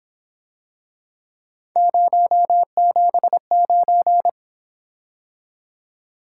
Morse code, heard as a keyed tone, about 27 wpm, 700 Hz. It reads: 079
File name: Speed-Racing - Three Letter-Number Combinations - 22wpm